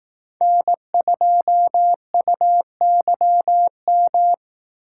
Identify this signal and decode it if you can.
Morse code, heard as a keyed tone, about 18 wpm, 700 Hz. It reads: N2UYM
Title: Speed Racing - USA 1x3 Call Signs 15wpm